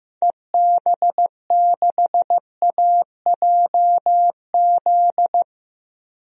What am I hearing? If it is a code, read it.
Morse code, heard as a keyed tone, about 15 wpm, 700 Hz. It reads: EB6AJZ
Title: DXCC-100 Call Signs - 6 Characters - Repeated 3x - 15wpm